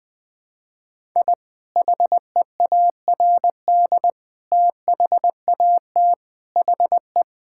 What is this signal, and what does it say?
Morse code, heard as a keyed tone, about 20 wpm, 700 Hz. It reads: I HEARD THAT HE
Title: Sentences with 4-Word Phrases - Spoken Beginning - 20wpm